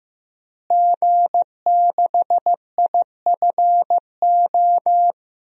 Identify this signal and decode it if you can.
Morse code, heard as a keyed tone, about 15 wpm, 700 Hz. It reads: G6IFO
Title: DXCC-100 Call Signs - 5 Characters - Repeated 3x - 15wpm